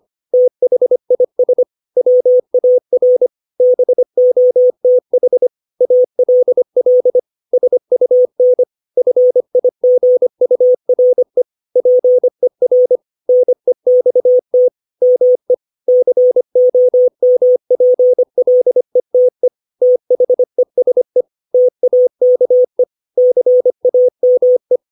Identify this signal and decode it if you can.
Morse code, heard as a keyed tone, about 25 wpm, 500 Hz. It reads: THIS WAR BOTH ALL SUN FIGURE PER NEXT ME COMPLETE THESE TAKE CAME
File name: 632 words / 632 Words 25WPM